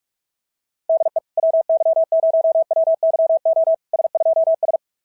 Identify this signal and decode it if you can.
Morse code, heard as a keyed tone, about 40 wpm, 650 Hz. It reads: DE WY0WYO S2S
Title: SOTA QSO Practice 45wpm